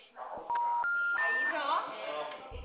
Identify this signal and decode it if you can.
Special information tone, with speech.